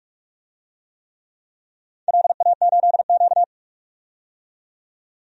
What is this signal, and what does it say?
Morse code, heard as a keyed tone, about 41 wpm, 700 Hz. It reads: LA8Q
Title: DXCC-100 Call Signs - 4 Characters - 45wpm